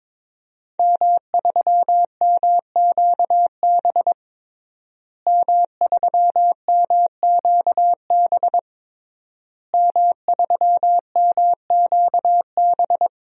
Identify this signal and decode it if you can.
Morse code, heard as a keyed tone, about 22 wpm, 700 Hz. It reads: M3MQB M3MQB M3MQB
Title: DXCC-100 Call Signs - 5 Characters - Repeated 3x - 22wpm